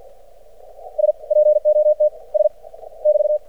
Morse code, heard as a keyed tone, about 52 wpm, spaced as for 35 wpm, 600 Hz. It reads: SPOTS <BT>
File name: archived / CRSnet